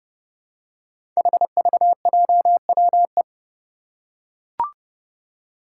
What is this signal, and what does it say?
Morse code, heard as a keyed tone, about 30 wpm, 700 Hz. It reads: HVJWE
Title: Five Challenging Letter-Number Combinations 30wpm